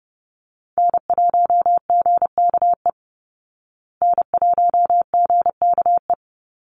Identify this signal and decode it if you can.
Morse code, heard as a keyed tone, about 30 wpm, 700 Hz. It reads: N1GKE N1GKE
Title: Call Signs Repeated 3x - 30wpm